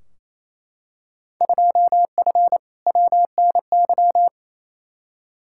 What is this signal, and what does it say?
Morse code, heard as a keyed tone, about 28 wpm, 700 Hz. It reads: 2F WNY
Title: Field Day Exchanges Encore - 28wpm